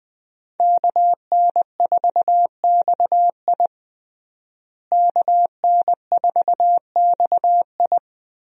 Morse code, heard as a keyed tone, about 20 wpm, 700 Hz. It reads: KN4XI KN4XI
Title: USA Call Signs - 5 Characters - Repeated 3x - 20wpm